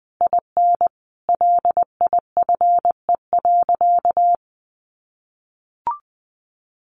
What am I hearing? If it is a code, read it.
Morse code, heard as a keyed tone, about 20 wpm, 700 Hz. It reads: IN LIFE.
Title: Sentences from Top 1200 Words 20wpm